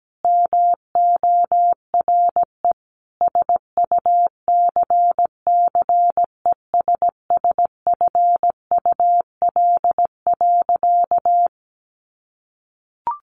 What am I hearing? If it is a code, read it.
Morse code, heard as a keyed tone, about 17 wpm, 700 Hz. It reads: MORE SUCCESSFUL.
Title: Sentences from Top 1500 Words 17wpm